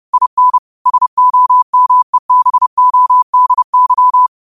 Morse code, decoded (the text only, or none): IN SOMEBODY